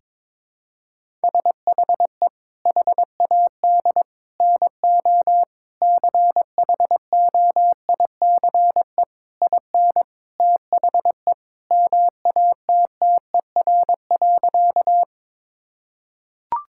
Morse code, heard as a keyed tone, about 22 wpm, 700 Hz. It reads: SHE HAD NO CHOICE IN THE MATTER.
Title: Sentences from Top 1000 Words - Review Part 6 - 22wpm